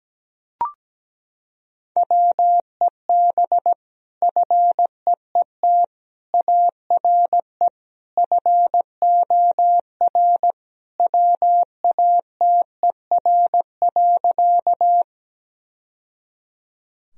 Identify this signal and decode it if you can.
Morse code, heard as a keyed tone, about 17 wpm, 700 Hz. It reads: WEB FEET ARE FOR WATER.
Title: Sentences from Top 1000 Words - Review Part 4 - 17wpm